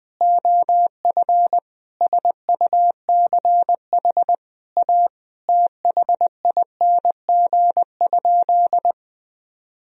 Morse code, heard as a keyed tone, about 20 wpm, 700 Hz. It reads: OF SUCH A THING?
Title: Sentences from Top 200 Words and Abbreviations 20wpm